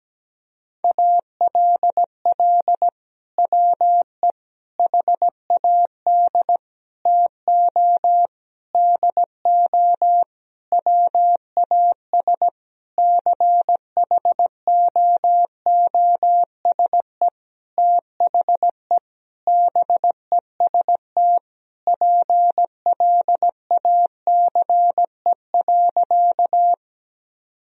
Morse code, heard as a keyed tone, about 17 wpm, 700 Hz. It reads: ALL WE HAD TO DO WAS CHOOSE THE BEST PLACE.
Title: Sentences from Top 1000 Words - Review Part 4 - 17wpm